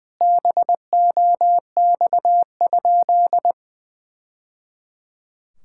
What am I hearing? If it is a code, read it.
Morse code, heard as a keyed tone, about 20 wpm, 700 Hz. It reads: BOX?
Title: Sentences from Top 1000 Words - Review Part 5 - 20wpm